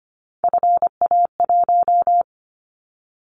Morse code, heard as a keyed tone, about 25 wpm, 700 Hz. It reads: FA1
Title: Callsign Prefixes for Top 50 DXCC - 25wpm